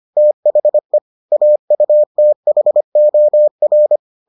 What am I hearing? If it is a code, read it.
Morse code, heard as a keyed tone, about 25 wpm, 600 Hz. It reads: THE AUTHOR